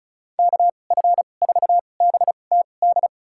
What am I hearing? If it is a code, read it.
Morse code, heard as a keyed tone, about 34 wpm, spaced as for 25 wpm, 700 Hz. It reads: KF4BTD
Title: USA Call Signs - 6 Characters - 2x Character Spacing - 35wpm